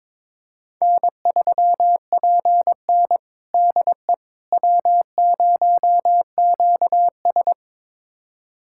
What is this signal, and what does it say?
Morse code, heard as a keyed tone, about 21 wpm, 700 Hz. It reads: N3PN DE W0QS
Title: USA Callsign DE Callsign - 4 Characters - 22wpm